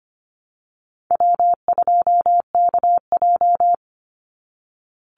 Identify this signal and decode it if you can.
Morse code, heard as a keyed tone, about 25 wpm, 700 Hz. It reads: W2KJ
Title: Call Signs - 25wpm